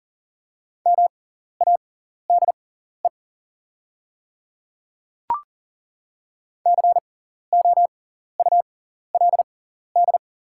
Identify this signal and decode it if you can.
Morse code, heard as a keyed tone, about 40 wpm, spaced as for 14 wpm, 700 Hz. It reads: MADE COULD
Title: 6x Character Spacing - Top 300 Words - 40wpm